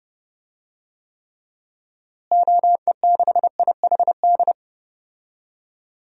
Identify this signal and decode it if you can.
Morse code, heard as a keyed tone, about 30 wpm, 700 Hz. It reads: OE6IHD
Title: DXCC-100 Call Signs - 6 Characters - Repeated 2x - 30wpm